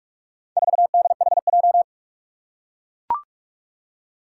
Morse code, heard as a keyed tone, about 44 wpm, 700 Hz. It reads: 4DHJ
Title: Four Challenging Letter-Number Combinations 45wpm